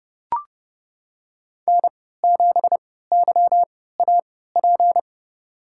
Morse code, heard as a keyed tone, about 32 wpm, spaced as for 17 wpm, 700 Hz. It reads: N7YAP
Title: USA Call Signs - 5 Characters - 3x Character Spacing - 30wpm